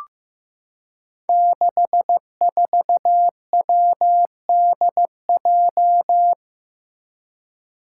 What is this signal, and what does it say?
Morse code, heard as a keyed tone, about 15 wpm, 700 Hz. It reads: 64WDJ
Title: Five Challenging Letter-Number Combinations 15wpm